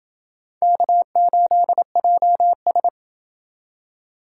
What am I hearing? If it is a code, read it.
Morse code, heard as a keyed tone, about 27 wpm, 700 Hz. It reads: K8JS